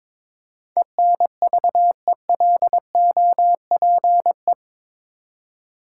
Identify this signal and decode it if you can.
Morse code, heard as a keyed tone, about 22 wpm, 700 Hz. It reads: ENVELOPE
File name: practice / 22wpm / Eight Letter Words - 22wpm